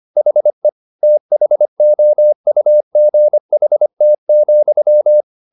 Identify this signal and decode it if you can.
Morse code, heard as a keyed tone, about 25 wpm, 600 Hz. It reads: HE THOUGHT,